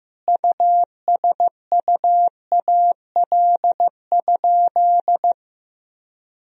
Morse code, heard as a keyed tone, about 15 wpm, 700 Hz. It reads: USUAL?
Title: Sentences from Top 1000 Words - Review Part 2 - 15wpm